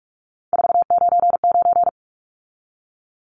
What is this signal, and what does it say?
Morse code, heard as a keyed tone, about 44 wpm, 700 Hz. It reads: 499
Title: Numbers - 3-Digit 45wpm